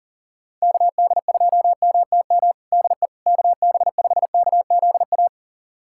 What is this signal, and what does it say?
Morse code, heard as a keyed tone, about 38 wpm, 690 Hz. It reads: KD2MTM DE KB5KZA
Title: USA Callsign DE Callsign - 6 Characters - 40wpm